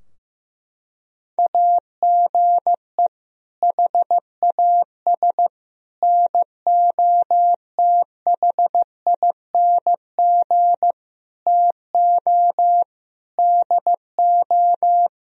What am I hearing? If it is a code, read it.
Morse code, heard as a keyed tone, about 15 wpm, 700 Hz. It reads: AGE HAS NOTHING TO DO
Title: Sentences from Top 1000 Words - Review Part 3 - 15wpm